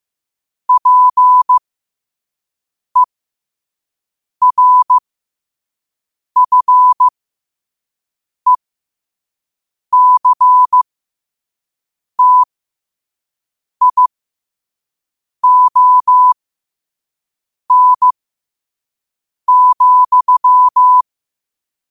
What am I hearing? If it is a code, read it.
Morse code, heard as a keyed tone, about 15 wpm, spaced as for 5 wpm, 1000 Hz. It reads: PERFECTION,
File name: qotdfiles / 5wpm